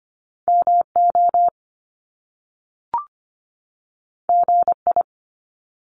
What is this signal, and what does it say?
Morse code, heard as a keyed tone, about 25 wpm, 700 Hz. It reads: MO GI
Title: Top Two Letter Combinations in Words 25wpm